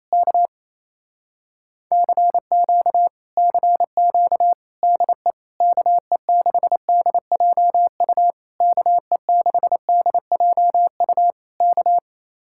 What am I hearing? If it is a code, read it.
Morse code, heard as a keyed tone, about 27 wpm, 700 Hz. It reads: K CQ CQ DE KE6DJU KE6DJU K
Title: Speed-Racing Common QSO Phrases - 28wpm